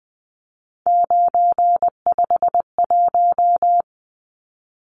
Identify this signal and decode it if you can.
Morse code, heard as a keyed tone, about 20 wpm, 700 Hz. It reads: 951
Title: Numbers - 3-Digit 20wpm